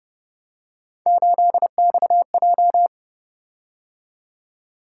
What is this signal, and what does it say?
Morse code, heard as a keyed tone, about 29 wpm, 700 Hz. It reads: 8XJ